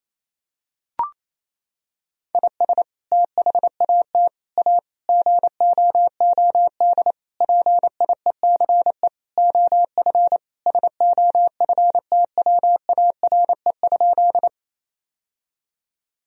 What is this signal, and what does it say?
Morse code, heard as a keyed tone, about 28 wpm, 700 Hz. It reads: IS THAT A GOOD PIECE OF SOFTWARE?
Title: Sentences from Top 1000 Words - Review Part 3 - 28wpm